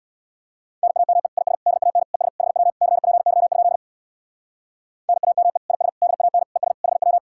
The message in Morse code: NAME IS DAN ES HW CPY? NAME IS DAN ES HW

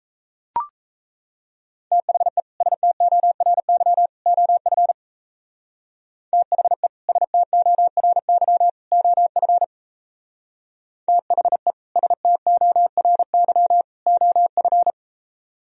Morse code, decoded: THE STORY OF THE STORY OF THE STORY OF

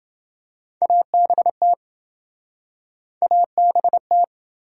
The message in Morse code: ABT ABT

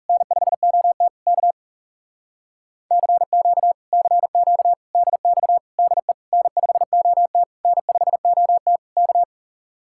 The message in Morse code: N5OT K CQ CQ DX DE N5OT N5OT K

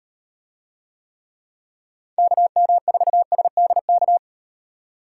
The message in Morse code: KM4SDK

